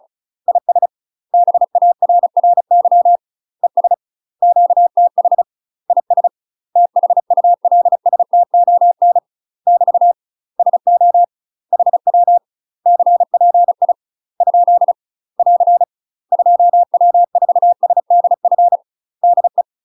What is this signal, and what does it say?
Morse code, heard as a keyed tone, about 34 wpm, 700 Hz. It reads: IS BARRY ES QTH IS THULSTON <BT> SO HW CPI ? <AR> 2W4SDF DE
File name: UK MASTER QSOs TRAINING / 35WPM / UK MASTER QSOs 0042